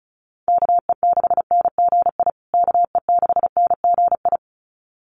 Morse code, heard as a keyed tone, about 34 wpm, 700 Hz. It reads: KE6NGI KE6NGI
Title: USA Call Signs Repeated 2x 35wpm